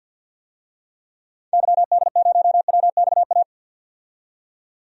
Morse code, heard as a keyed tone, about 52 wpm, 700 Hz. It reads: YD0WXA
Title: DXCC-100 Call Signs - 6 Characters - 50wpm